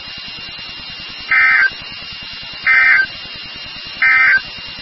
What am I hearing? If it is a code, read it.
Emergency Alert System (SAME) tones, heard as harsh buzzing bursts. It is heard from 1.2 to 4.5 s.
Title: NOAA Weather Radio WXK36